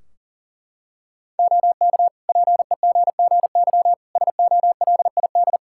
Morse code, heard as a keyed tone, about 40 wpm, 700 Hz. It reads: OK PEGGY SOLID
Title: Practice QSO - Exchange 3 - More Info - 40wpm